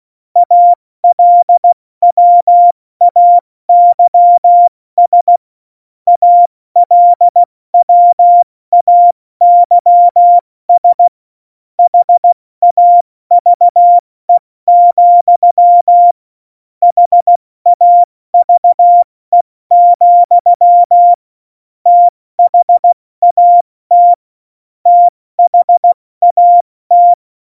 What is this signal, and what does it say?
Morse code, heard as a keyed tone, about 16 wpm, 700 Hz. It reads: ALWAYS ALWAYS HAVE, HAVE, THAT THAT
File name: Morsekurs / Witze / Witz 25 Juli 17, Tempo 70 F80